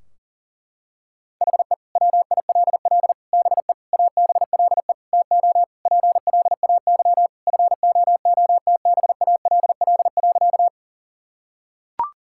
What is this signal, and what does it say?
Morse code, heard as a keyed tone, about 37 wpm, 700 Hz. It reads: HE WILL BE ABLE TO PLAY FOOTBALL.
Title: Sentences from Top 1600 Words 40wpm